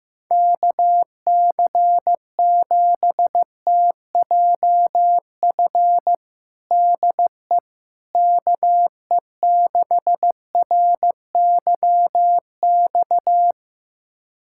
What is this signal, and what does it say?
Morse code, heard as a keyed tone, about 15 wpm, 700 Hz. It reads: KC7TJF DE KE6RYX
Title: USA Callsign DE Callsign - 6 Characters - 15wpm